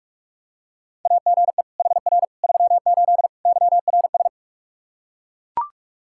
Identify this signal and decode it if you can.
Morse code, heard as a keyed tone, about 41 wpm, 700 Hz. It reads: AGE HR 38 YRS
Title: K1USN SST QSO Practice 45wpm